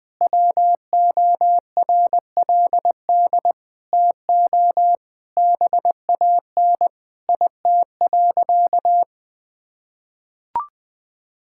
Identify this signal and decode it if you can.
Morse code, heard as a keyed tone, about 20 wpm, 700 Hz. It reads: WORLD TO BAN IT.
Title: Sentences with 8-Word Phrases - 20wpm